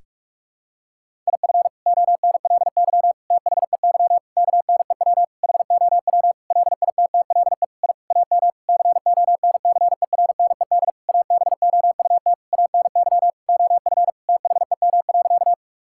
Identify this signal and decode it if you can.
Morse code, heard as a keyed tone, about 46 wpm, 700 Hz. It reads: IF ONLY THEY KNEW HOW LITTLE I AM CONCERNED ABOUT ANY OF THEM.
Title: Sentences from Top 1000 Words - Review Part 6 - 45wpm